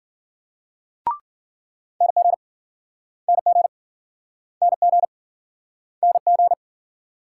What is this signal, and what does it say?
Morse code, heard as a keyed tone, about 53 wpm, 700 Hz. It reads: NG NG NG NG